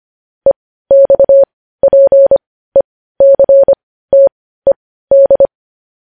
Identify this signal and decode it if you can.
Morse code, heard as a keyed tone, about 25 wpm, spaced as for 15 wpm, 550 Hz. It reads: EXPECTED
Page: News Headlines in Morse Code at 15 WPM